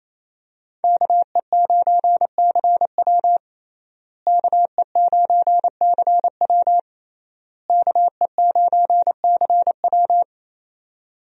Morse code, heard as a keyed tone, about 27 wpm, 700 Hz. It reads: KE9CW KE9CW KE9CW
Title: USA Call Signs - 5 Characters - Repeated 3x - 28wpm